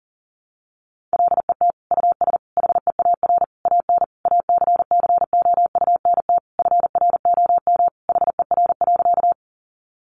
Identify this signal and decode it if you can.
Morse code, heard as a keyed tone, about 40 wpm, 700 Hz. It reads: LET US HEAR AN ACCOUNT FROM HER.
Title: Sentences from Top 900 Words 40wpm